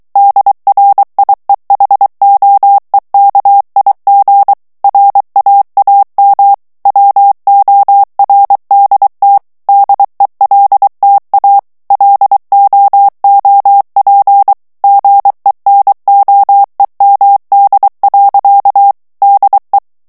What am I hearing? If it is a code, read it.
Morse code, heard as a keyed tone, about 23 wpm, 800 Hz. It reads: DRIEHOEKIG RAAM WORDT DELTA LOOP GENOEMD. DE
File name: HOLLANDAIS texte clair 1 vitesse 22 mots minute